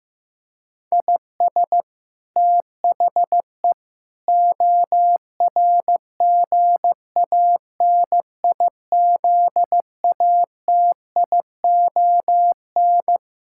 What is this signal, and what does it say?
Morse code, heard as a keyed tone, about 15 wpm, 700 Hz. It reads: IS THE ORGANIZATION
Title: Sentences from Top 1000 Words - Review Part 5 - 15wpm